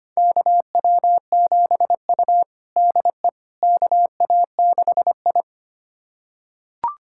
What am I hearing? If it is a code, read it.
Morse code, heard as a keyed tone, about 25 wpm, 700 Hz. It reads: KW7U DE KA6I
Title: USA Callsign DE Callsign - 4 Characters - 25wpm